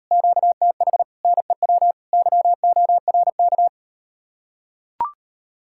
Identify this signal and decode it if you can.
Morse code, heard as a keyed tone, about 39 wpm, 700 Hz. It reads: QTH NEW YORK